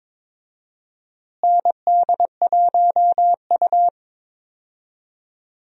Morse code, heard as a keyed tone, about 22 wpm, 700 Hz. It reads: ND1U
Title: USA Call Signs - 4 Characters - Repeated 2x - 22wpm